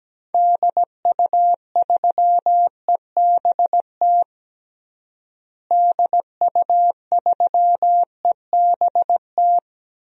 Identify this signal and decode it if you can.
Morse code, heard as a keyed tone, about 17 wpm, 700 Hz. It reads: DU3EBT DU3EBT
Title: DXCC-100 Call Signs - 6 Characters - Repeated 2x - 17wpm